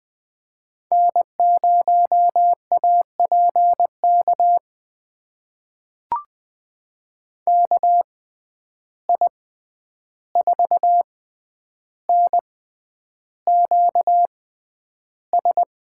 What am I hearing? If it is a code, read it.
Morse code, heard as a keyed tone, about 20 wpm, spaced as for 10 wpm, 700 Hz. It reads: N0APK KI4NQS
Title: USA Call Signs - 6x Character Spacing - 20wpm